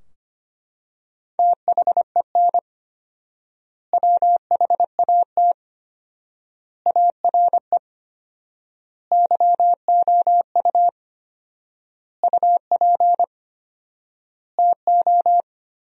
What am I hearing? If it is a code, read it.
Morse code, heard as a keyed tone, about 25 wpm, 700 Hz. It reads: THEN WHAT ARE YOU UP TO
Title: 4x Word-Spacing - Sentences from Top 100 Words - 25wpm